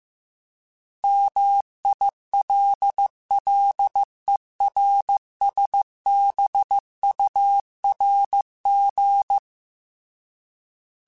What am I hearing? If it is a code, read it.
Morse code, heard as a keyed tone, about 15 wpm, 800 Hz. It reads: MILLERSBURG